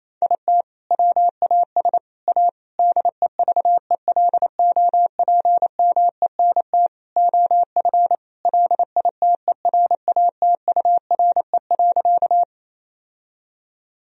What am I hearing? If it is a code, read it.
Morse code, heard as a keyed tone, about 28 wpm, 700 Hz. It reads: IT WAS A DEVELOPMENT OF LITERATURE.
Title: Sentences from Top 1000 Words - Review Part 4 - 28wpm